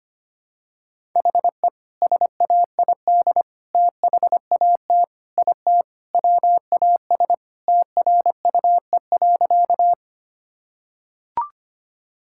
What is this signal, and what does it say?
Morse code, heard as a keyed tone, about 25 wpm, 700 Hz. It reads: HE SAID THAT IT WAS TRUE.
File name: practice / 25wpm / Sentences with 2-Word Phrases - 25wpm